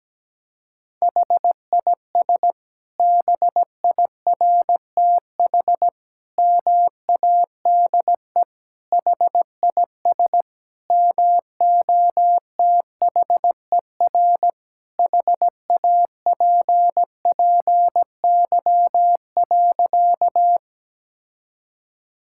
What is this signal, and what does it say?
Morse code, heard as a keyed tone, about 17 wpm, 700 Hz. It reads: HIS BIRTH MADE HIS MOTHER HAPPY.
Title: Sentences from Top 1000 Words - Review Part 5 - 17wpm